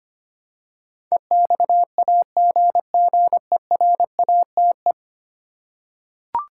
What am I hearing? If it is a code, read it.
Morse code, heard as a keyed tone, about 25 wpm, 700 Hz. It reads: EXAGGERATE
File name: Ten Letter Words - 25wpm